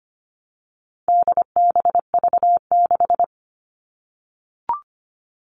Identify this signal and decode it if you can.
Morse code, heard as a keyed tone, about 25 wpm, 700 Hz. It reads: DBV6
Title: All Cut Numbers - 4-Digit 25wpm